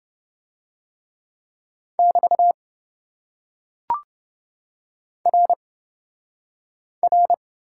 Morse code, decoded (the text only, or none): <BT> R R